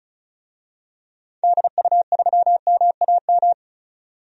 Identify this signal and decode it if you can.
Morse code, heard as a keyed tone, about 34 wpm, 700 Hz. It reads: DU3MAM
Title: DXCC-100 Call Signs - 6 Characters - 35wpm